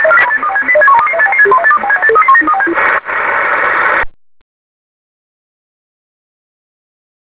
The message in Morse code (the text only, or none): none